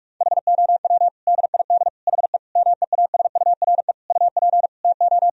Morse code, heard as a keyed tone, about 46 wpm, 700 Hz. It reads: HOW DID HE MEASURE UP TO
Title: Sentences from Top 1000 Words - Review Part 2 - 45wpm